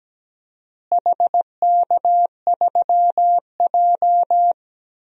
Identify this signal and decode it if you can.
Morse code, heard as a keyed tone, about 17 wpm, 700 Hz. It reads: HK3J
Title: DXCC-100 Call Signs - 4 Characters - Repeated 3x - 17wpm